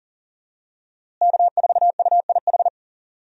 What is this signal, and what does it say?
Morse code, heard as a keyed tone, about 40 wpm, 700 Hz. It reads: K4UIH
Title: USA Call Signs - 5 Characters - Repeated 2x - 40wpm